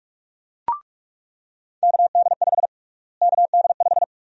Morse code, heard as a keyed tone, about 44 wpm, 700 Hz. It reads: KD5 KD5
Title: USA Call Sign Prefixes - Repeated 2x - 45wpm